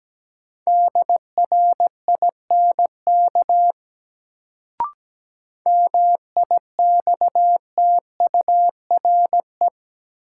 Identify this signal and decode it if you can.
Morse code, heard as a keyed tone, about 17 wpm, 700 Hz. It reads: DRINK MIXTURE
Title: Sentences from Top 1500 Words - New Word Drill - 17wpm